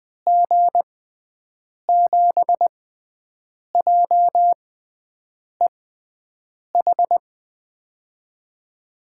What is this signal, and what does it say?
Morse code, heard as a keyed tone, about 20 wpm, spaced as for 7 wpm, 700 Hz. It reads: G7JEH
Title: USA Call Signs - 6 Characters - 6x Character Spacing - 20wpm